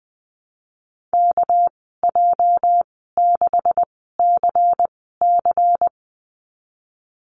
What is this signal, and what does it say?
Morse code, heard as a keyed tone, about 20 wpm, spaced as for 15 wpm, 700 Hz. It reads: KJ6CC
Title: 238 Lesson 38g - Letter q - Call Signs 20wpm